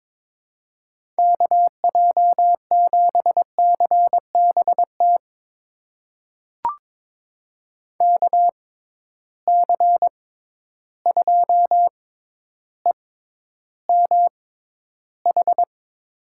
Morse code, heard as a keyed tone, about 22 wpm, spaced as for 11 wpm, 700 Hz. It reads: KJ7CBT KC2EMH
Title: USA Call Signs - 6 Characters - 6x Character Spacing - 22wpm